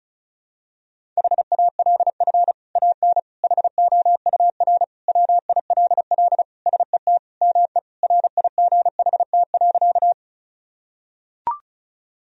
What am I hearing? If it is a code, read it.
Morse code, heard as a keyed tone, about 36 wpm, 700 Hz. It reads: HALF AN HOUR WILL SET ME RIGHT.
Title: Sentences from Top 1000 Words - Review Part 1 - 35wpm